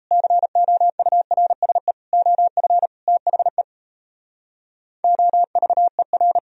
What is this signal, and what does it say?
Morse code, heard as a keyed tone, about 38 wpm, 700 Hz. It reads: COURSE OF THE OVER